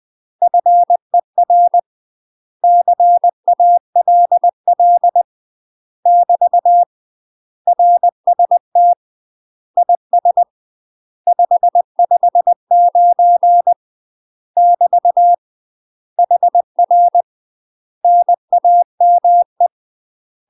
Morse code, heard as a keyed tone, about 20 wpm, 700 Hz. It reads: FER CALL <BT> RST IS 559 <BT> HR NAME
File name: UK MASTER QSOs TRAINING / 20WPM / UK MASTER QSOs 0044